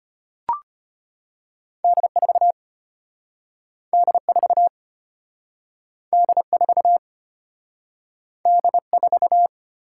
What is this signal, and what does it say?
Morse code, heard as a keyed tone, about 34 wpm, 700 Hz. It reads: D4 D4 D4 D4